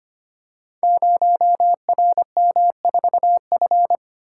Morse code, heard as a keyed tone, about 25 wpm, 700 Hz. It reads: 0RM4F
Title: Code Groups - 5 Letter:Numbers - 25wpm